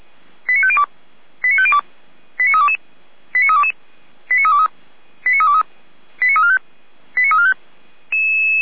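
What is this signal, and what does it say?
Five-tone selective call (ZVEI), heard as quick runs of five tones, tones of about 70 ms. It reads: 89502 89502 89322 89322 89323 89323 89345 89345
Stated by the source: Five-tone ZVEI selective call, received with an SDR.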